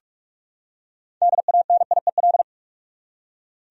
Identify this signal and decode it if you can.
Morse code, heard as a keyed tone, about 46 wpm, 700 Hz. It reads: DANIEL
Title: USA Common Names - 45wpm